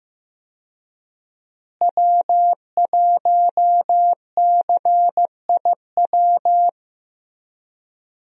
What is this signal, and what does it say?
Morse code, heard as a keyed tone, about 15 wpm, 700 Hz. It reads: W1CIW
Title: USA Call Signs - 5 Characters - Repeated 2x - 15wpm